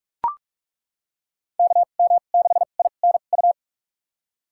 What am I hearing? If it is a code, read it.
Morse code, heard as a keyed tone, about 42 wpm, spaced as for 32 wpm, 700 Hz. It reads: KM6INU
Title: USA Call Signs - 6 Characters - 2x Character Spacing - 45wpm